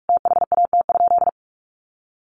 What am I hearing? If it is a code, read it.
Morse code, heard as a keyed tone, about 44 wpm, 700 Hz. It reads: THAT?
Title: Sentences from Top 1200 Words 45wpm